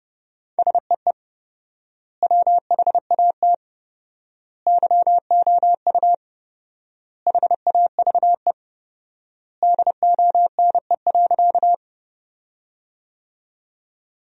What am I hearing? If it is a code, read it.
Morse code, heard as a keyed tone, about 31 wpm, 700 Hz. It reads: SEE WHAT YOU HAVE DONE.
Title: 4x Word-Spacing - Sentences from Top 500 Words - 30wpm